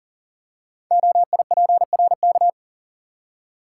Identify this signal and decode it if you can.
Morse code, heard as a keyed tone, about 38 wpm, 690 Hz. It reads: OIPRK
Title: Code Groups - 5 Letters - 40wpm